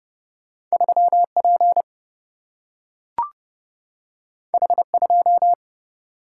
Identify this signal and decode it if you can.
Morse code, heard as a keyed tone, about 30 wpm, 700 Hz. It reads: 3P H2
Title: Single-Letter Single-Number 30wpm